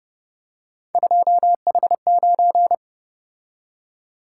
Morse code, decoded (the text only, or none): 2H9